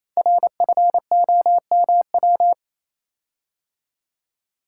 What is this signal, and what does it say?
Morse code, heard as a keyed tone, about 29 wpm, 700 Hz. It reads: RFOMW
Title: Code Groups - 5 Letters - 28wpm